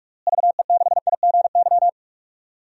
Morse code, heard as a keyed tone, about 46 wpm, 690 Hz. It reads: VE6IGY
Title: DXCC-100 Call Signs - 6 Characters - Repeated 2x - 45wpm